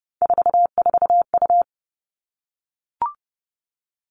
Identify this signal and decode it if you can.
Morse code, heard as a keyed tone, about 30 wpm, 700 Hz. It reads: 44U
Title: All Cut Numbers - 3-Digit 30wpm